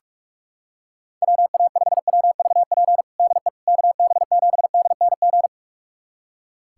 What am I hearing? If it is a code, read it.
Morse code, heard as a keyed tone, about 42 wpm, 700 Hz. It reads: WA5WVP DE KB7DNG
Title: USA Callsign DE Callsign - 6 Characters - 45wpm